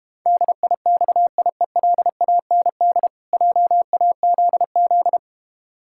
Morse code, heard as a keyed tone, about 32 wpm, 700 Hz. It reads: DIXIELAND JAZZ